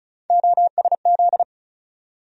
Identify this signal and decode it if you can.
Morse code, heard as a keyed tone, about 34 wpm, 700 Hz. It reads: OSZ
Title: USA Call Sign Postfixes - 3 Letters 35wpm